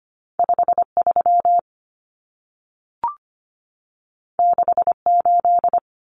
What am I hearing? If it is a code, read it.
Morse code, heard as a keyed tone, about 25 wpm, 700 Hz. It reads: 53 68
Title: Common Cut Numbers - 2-Digit 25wpm